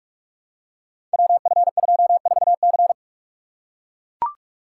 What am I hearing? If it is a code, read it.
Morse code, heard as a keyed tone, about 41 wpm, 700 Hz. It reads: WF24C
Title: Code Groups - 5 Letter:Numbers - 45wpm